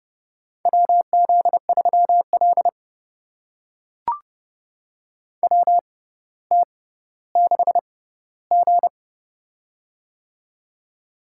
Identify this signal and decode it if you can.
Morse code, heard as a keyed tone, about 30 wpm, spaced as for 15 wpm, 700 Hz. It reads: WZ3L WT6G
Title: USA Call Signs - 4 Characters - 6x Character Spacing - 30wpm